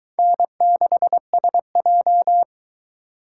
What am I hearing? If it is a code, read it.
Morse code, heard as a keyed tone, about 22 wpm, 700 Hz. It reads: N6SJ